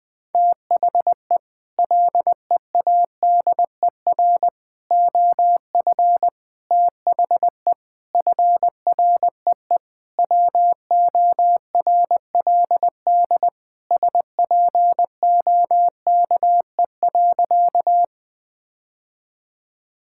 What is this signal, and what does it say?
Morse code, heard as a keyed tone, about 20 wpm, 700 Hz. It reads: THE LEADER OF THE FREE WORLD SPOKE.
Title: Sentences with 6-Word Phrases - Spoken Beginning - 20wpm